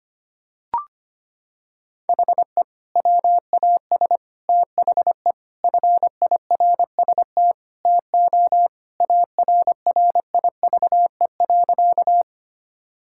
Morse code, heard as a keyed tone, about 25 wpm, 700 Hz. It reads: HE WAS THE FIRST TO ARRIVE.
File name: practice / 25wpm / Sentences with 2-Word Phrases - 25wpm